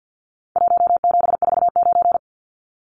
Morse code, heard as a keyed tone, about 50 wpm, 700 Hz. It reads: 1749
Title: Numbers - Contest Style - 4-Digit 50wpm